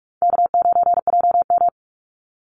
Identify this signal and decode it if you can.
Morse code, heard as a keyed tone, about 46 wpm, 700 Hz. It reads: K9JM
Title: Easy Call Signs - 45wpm